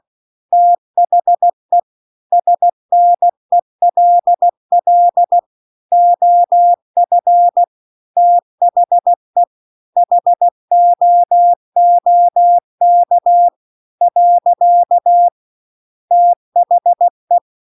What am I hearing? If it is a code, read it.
Morse code, heard as a keyed tone, about 16 wpm, 700 Hz. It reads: THE SNELL OF THE HOOK . THE
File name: CW-The-Big-Twoheaded-River-2-16WPM